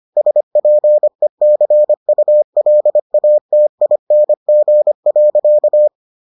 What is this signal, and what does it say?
Morse code, heard as a keyed tone, about 25 wpm, 600 Hz. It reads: SPECULATING.